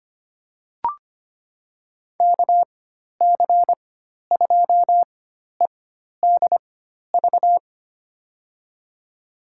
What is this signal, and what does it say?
Morse code, heard as a keyed tone, about 25 wpm, spaced as for 12 wpm, 700 Hz. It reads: KC2EDV
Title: USA Call Signs - 4x Character Spacing - 25wpm